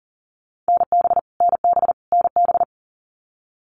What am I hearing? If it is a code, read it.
Morse code, heard as a keyed tone, about 40 wpm, 700 Hz. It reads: NB NB NB
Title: Canadian Provinces and Territories - Repeated 3x - 40wpm